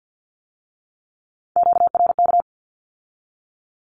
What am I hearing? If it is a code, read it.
Morse code, heard as a keyed tone, about 50 wpm, 700 Hz. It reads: QRK
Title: Q-Codes with Spoken Meaning - 50wpm